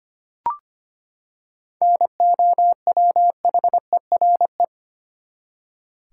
Morse code, heard as a keyed tone, about 25 wpm, 700 Hz. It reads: NOWHERE
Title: Seven Letter Words - 25wpm